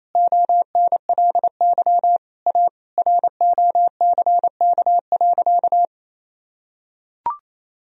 Morse code, heard as a keyed tone, about 28 wpm, 690 Hz. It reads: ONLY A ROCK.
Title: Sentences from Top 1000 Words - Review Part 2 - 28wpm